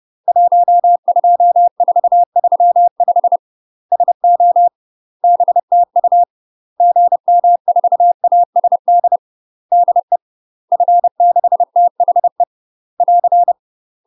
Morse code, heard as a keyed tone, about 30 wpm, 700 Hz. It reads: 12435 SO BTU GM4ASD DE F6THE <AR>
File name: UK MASTER QSOs TRAINING / 30WPM / UK MASTER QSOs 0020